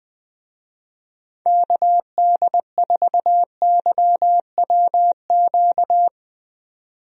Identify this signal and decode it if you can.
Morse code, heard as a keyed tone, about 20 wpm, 700 Hz. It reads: KD4YWQ
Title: USA Call Signs - 6 Characters - 20wpm